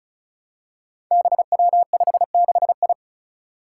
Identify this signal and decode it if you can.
Morse code, heard as a keyed tone, about 36 wpm, 700 Hz. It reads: BW56I
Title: Five Challenging Letter-Number Combinations 35wpm